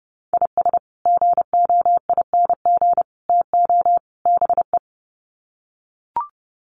Morse code, heard as a keyed tone, about 30 wpm, 700 Hz. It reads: IS GOING TO BE
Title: Sets of 4 Words 30wpm